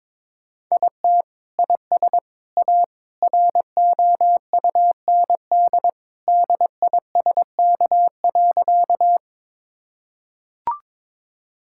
Morse code, heard as a keyed tone, about 22 wpm, 700 Hz. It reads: IT IS A ROUND DISK.
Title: Sentences from Top 1000 Words - Review Part 6 - 22wpm